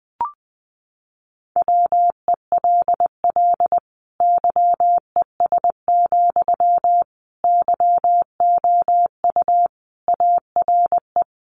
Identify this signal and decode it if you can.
Morse code, heard as a keyed tone, about 19 wpm, 700 Hz. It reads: WELL YES, YOU ARE
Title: Sentences from Top 500 Words 20wpm